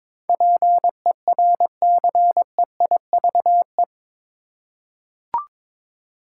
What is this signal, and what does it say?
Morse code, heard as a keyed tone, about 21 wpm, 700 Hz. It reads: PERCEIVE
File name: practice / 22wpm / Eight Letter Words - 22wpm